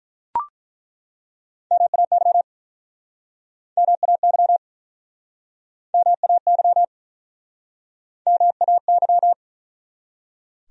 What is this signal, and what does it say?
Morse code, heard as a keyed tone, about 47 wpm, 700 Hz. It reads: MAY MAY MAY MAY